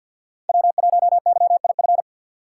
Morse code, heard as a keyed tone, about 52 wpm, 700 Hz. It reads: W1YIF
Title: USA Call Signs - 2x Character Spacing - 50wpm